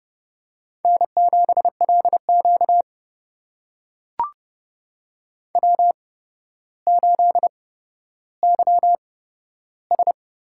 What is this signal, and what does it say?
Morse code, heard as a keyed tone, about 30 wpm, spaced as for 13 wpm, 700 Hz. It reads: N7LQ W8YS
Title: USA Call Signs - 4 Characters - 8x Character Spacing - 30wpm